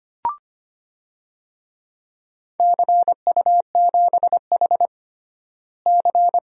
Morse code, heard as a keyed tone, about 25 wpm, 700 Hz. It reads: CU7H C
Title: DXCC-100 Call Signs - 4 Characters - Repeated 3x - 25wpm